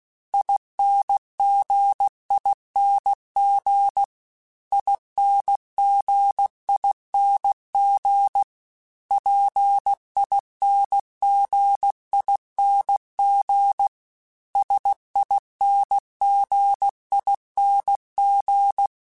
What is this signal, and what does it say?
Morse code, heard as a keyed tone, about 16 wpm, 800 Hz. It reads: INGING INGING PINGING SINGING
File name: Special INGs-mp3s / Special INGs-mp3s / Dble Ings @15